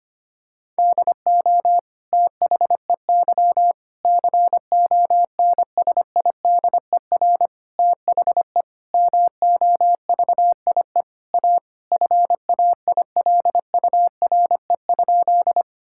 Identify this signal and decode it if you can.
Morse code, heard as a keyed tone, about 25 wpm, 700 Hz. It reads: DO THEY CONSIDER THE MOVIE A FAILURE?
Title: Sentences from Top 1000 Words - Review Part 6 - 25wpm